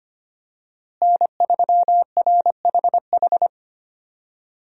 Morse code, heard as a keyed tone, about 25 wpm, 700 Hz. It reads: N3RHH
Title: USA Call Signs - 5 Characters - Repeated 2x - 25wpm